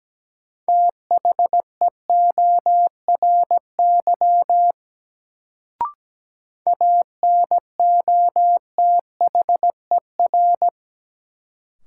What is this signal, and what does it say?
Morse code, heard as a keyed tone, about 17 wpm, 700 Hz. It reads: THEORY ANOTHER
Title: Sentences from Top 600 Words - New Word Drill - 17wpm